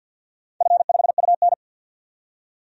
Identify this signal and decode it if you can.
Morse code, heard as a keyed tone, about 47 wpm, 700 Hz. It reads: F5UN
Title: DXCC-100 Call Signs - 4 Characters - 50wpm